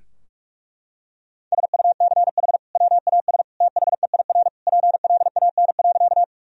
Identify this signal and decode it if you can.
Morse code, heard as a keyed tone, about 46 wpm, 700 Hz. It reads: SUCH WAS THEIR PLAN.
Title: Sentences from Top 1000 Words - Review Part 2 - 45wpm